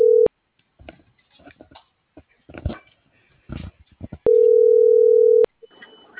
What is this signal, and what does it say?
Ringback tone (North America), 440 + 480 Hz.